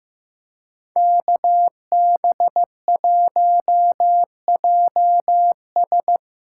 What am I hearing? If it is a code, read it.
Morse code, heard as a keyed tone, about 15 wpm, 700 Hz. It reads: KB1JS
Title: USA Call Signs - 2x Character Spacing - 15wpm